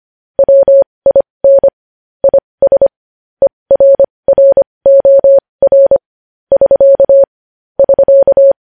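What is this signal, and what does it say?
Morse code, heard as a keyed tone, about 25 wpm, spaced as for 20 wpm, 550 Hz. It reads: WIN IS ERROR <SK> <SK>
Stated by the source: Morse code news headlines sent at 20.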